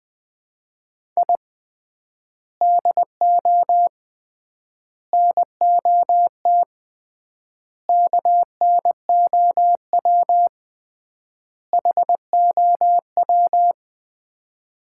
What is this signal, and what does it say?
Morse code, heard as a keyed tone, about 20 wpm, 700 Hz. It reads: I DO NOT KNOW HOW
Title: 3x Word-Spacing - Sets of 5 Words - 20wpm